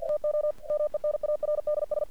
Morse code, heard as a keyed tone, about 48 wpm, 600 Hz. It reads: TO MENARDS
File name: archived / CRSnet